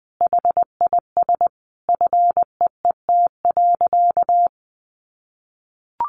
Morse code, heard as a keyed tone, about 20 wpm, 700 Hz. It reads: HIS FEET.
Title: A Sound Of Thunder 20wpm